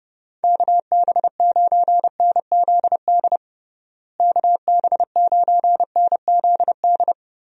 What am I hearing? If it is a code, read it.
Morse code, heard as a keyed tone, about 32 wpm, 700 Hz. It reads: KB9NZD KB9NZD
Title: USA Call Signs - 6 Characters - Repeated 2x - 30wpm